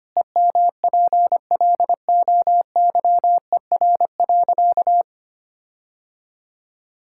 Morse code, heard as a keyed tone, about 25 wpm, 700 Hz. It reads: EMPLOYER.
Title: Sentences from Top 1600 Words 25wpm